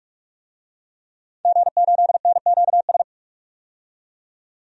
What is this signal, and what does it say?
Morse code, heard as a keyed tone, about 46 wpm, 700 Hz. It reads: G8NQS
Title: DXCC-100 Call Signs - 5 Characters - 45wpm